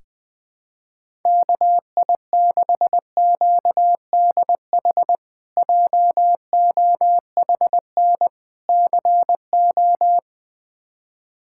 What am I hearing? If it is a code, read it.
Morse code, heard as a keyed tone, about 21 wpm, 700 Hz. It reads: KI6QDH JOHN CO
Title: K1USN SST QSO Practice - 20wpm